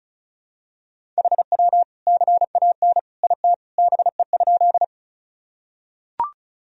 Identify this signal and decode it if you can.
Morse code, heard as a keyed tone, about 36 wpm, 700 Hz. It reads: HW CAN IT BE?
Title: Sentences from Top 100 Words and Abbreviations 35wpm